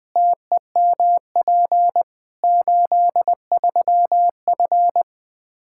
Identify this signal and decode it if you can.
Morse code, heard as a keyed tone, about 20 wpm, 700 Hz. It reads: TEMP 83F
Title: Practice QSO - Exchange 2 - Info - 20wpm